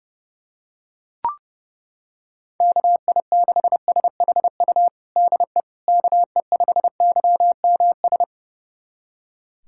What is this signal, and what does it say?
Morse code, heard as a keyed tone, about 30 wpm, 700 Hz. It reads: KI6SHU DE KE5YMS
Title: USA Callsign DE Callsign - 6 Characters - 30wpm